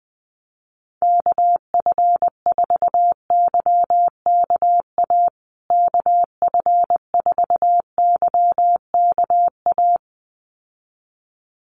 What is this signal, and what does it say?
Morse code, heard as a keyed tone, about 20 wpm, 700 Hz. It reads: KF4YKA KF4YKA
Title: USA Call Signs Repeated 2x 20wpm